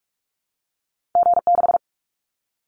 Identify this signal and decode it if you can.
Morse code, heard as a keyed tone, about 44 wpm, 700 Hz. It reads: G6
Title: Callsign Prefixes for Top 25 DXCC - 45wpm